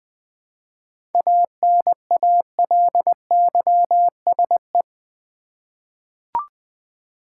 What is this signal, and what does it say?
Morse code, heard as a keyed tone, about 20 wpm, 700 Hz. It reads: ANALYSE
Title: 7 Letter Words - First 3 Letters Given - 20wpm